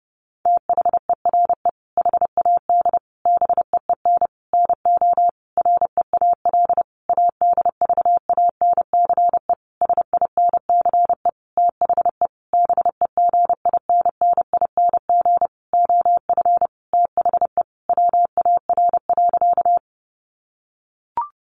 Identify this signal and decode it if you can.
Morse code, heard as a keyed tone, about 28 wpm, 700 Hz. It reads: THERE HAD BEEN NO REAL ADVANCE SINCE THE BEGINNING OF THE WAR.
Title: Sentences from Top 1400 Words 30wpm